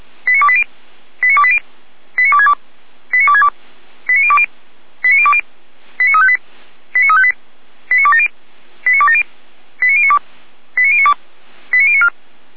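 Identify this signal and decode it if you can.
Five-tone selective call (ZVEI), heard as quick runs of five tones, tones of about 70 ms. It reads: 89280 89280 89262 89262 89020 89020 89358 89358 89270 89270 89092 89092 89094
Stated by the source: ZVEI five-tone selective call received by SDR.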